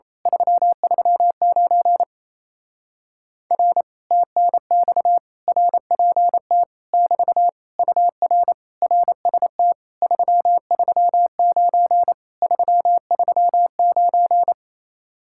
Morse code, heard as a keyed tone, about 27 wpm, 700 Hz. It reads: 339 R TNX RPT <BT> UR RST 339 339
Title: Speed-Racing Common QSO Phrases - 28wpm